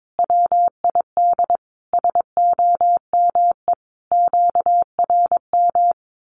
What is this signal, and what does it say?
Morse code, heard as a keyed tone, about 22 wpm, 700 Hz. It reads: WID SOME QRM
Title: Common QSO Phrases 22wpm